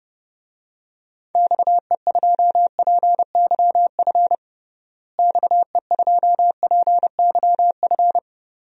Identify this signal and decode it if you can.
Morse code, heard as a keyed tone, about 30 wpm, 700 Hz. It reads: XE2PYF XE2PYF
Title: DXCC-100 Call Signs - 6 Characters - Repeated 2x - 30wpm